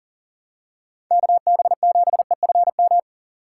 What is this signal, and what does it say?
Morse code, heard as a keyed tone, about 40 wpm, 700 Hz. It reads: KB7EFM
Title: USA Call Signs - 6 Characters - 4x Character Spacing - 40wpm